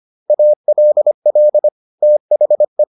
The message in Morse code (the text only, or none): ALL THE